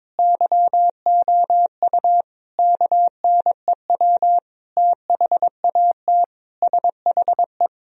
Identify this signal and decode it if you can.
Morse code, heard as a keyed tone, about 22 wpm, 700 Hz. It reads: YOU KNEW THAT SHE
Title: Sentences from Top 1000 Words - Review Part 2 - 22wpm